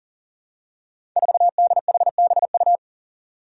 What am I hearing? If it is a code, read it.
Morse code, heard as a keyed tone, about 40 wpm, 700 Hz. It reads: 4DHBU
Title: Five Challenging Letter-Number Combinations 40wpm